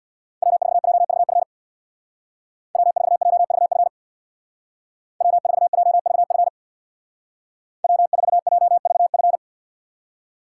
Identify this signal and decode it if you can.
Morse code, heard as a keyed tone, about 65 wpm, 700 Hz. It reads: W4JVF W4JVF W4JVF W4JVF